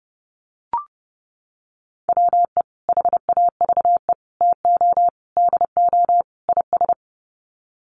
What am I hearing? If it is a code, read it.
Morse code, heard as a keyed tone, about 30 wpm, 700 Hz. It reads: WE HAVE TO DO IS
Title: Sets of 5 Words Encore 30wpm